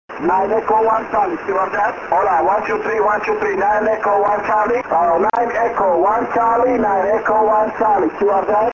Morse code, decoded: none